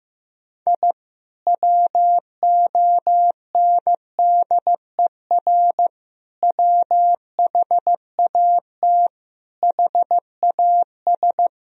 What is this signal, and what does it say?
Morse code, heard as a keyed tone, about 15 wpm, 700 Hz. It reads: I WONDER WHAT HAS
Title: Sentences from Top 1000 Words - Review Part 4 - 15wpm